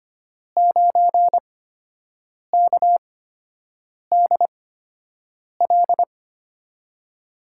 Morse code, heard as a keyed tone, about 25 wpm, spaced as for 7 wpm, 700 Hz. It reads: 9KDL
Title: USA Call Signs - 5 Characters - 8x Character Spacing - 25wpm